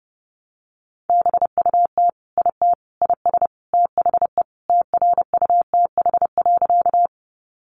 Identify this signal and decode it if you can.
Morse code, heard as a keyed tone, about 30 wpm, 700 Hz. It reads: BUT IT IS THE TRUTH.
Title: Sentences from Top 800 Words 30wpm